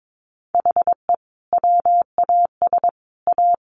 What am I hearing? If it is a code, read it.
Morse code, heard as a keyed tone, about 22 wpm, 700 Hz. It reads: HE WAS A
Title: Sets of 3 Words 22wpm